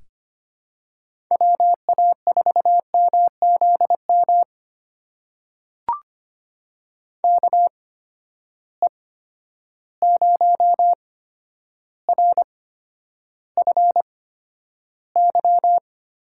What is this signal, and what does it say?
Morse code, heard as a keyed tone, about 25 wpm, spaced as for 11 wpm, 700 Hz. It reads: WA4MZM KE0RFY
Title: USA Call Signs - 6 Characters - 8x Character Spacing - 25wpm